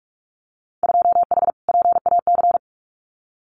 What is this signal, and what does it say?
Morse code, heard as a keyed tone, about 44 wpm, 690 Hz. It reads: 2H PAC
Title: Winter Field Day Exchanges Encore 45wpm